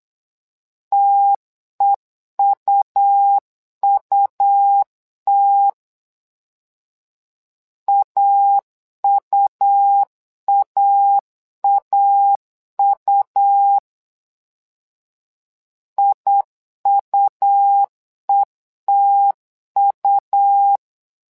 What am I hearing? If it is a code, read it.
Morse code, heard as a keyed tone, about 8 wpm, 800 Hz. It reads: TEUUT AUAAU IUETU
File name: LEZIONI / 2 UHDN / 2 U 8wpm 2028